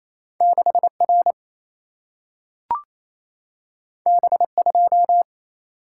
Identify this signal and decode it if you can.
Morse code, heard as a keyed tone, about 28 wpm, 700 Hz. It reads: 6 R B 2
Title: Single-Letter Single-Number 28wpm